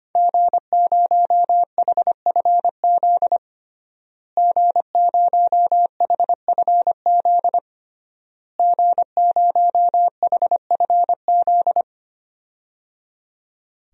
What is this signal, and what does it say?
Morse code, heard as a keyed tone, about 25 wpm, 700 Hz. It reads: G0HFZ G0HFZ G0HFZ
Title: DXCC-100 Call Signs - 5 Characters - Repeated 3x - 25wpm